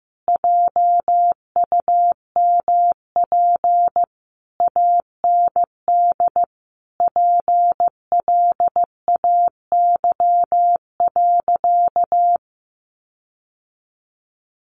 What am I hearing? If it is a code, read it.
Morse code, heard as a keyed tone, about 15 wpm, 700 Hz. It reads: JUMP AND PLAY.
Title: Sentences from Top 1300 Words 15wpm